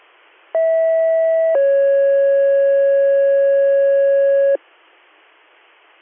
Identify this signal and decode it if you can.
Two-tone page, heard as a one-second tone, then a three-second tone, about 650 then 550 Hz. It is heard from 0.5 to 4.6 s.